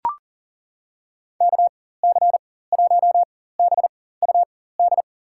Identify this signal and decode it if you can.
Morse code, heard as a keyed tone, about 40 wpm, spaced as for 19 wpm, 700 Hz. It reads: KC1BUD
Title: USA Call Signs - 6 Characters - 4x Character Spacing - 40wpm